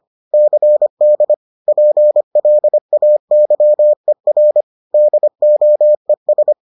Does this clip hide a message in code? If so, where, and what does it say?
Yes: Morse code from 0.3 to 6.5 s, heard as a keyed tone, about 25 wpm, 600 Hz. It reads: CD PLAYER DOES